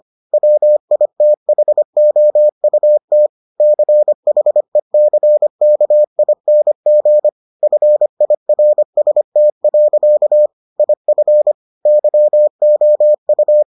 Morse code, heard as a keyed tone, about 25 wpm, 600 Hz. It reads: WITHOUT CHECKING FIRST. IF YOU